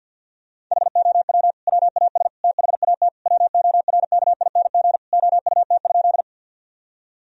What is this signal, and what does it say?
Morse code, heard as a keyed tone, about 50 wpm, 700 Hz. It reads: HOW WAS THAT WORKING OUT?
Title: Sentences from Top 1000 Words - Review Part 4 - 50wpm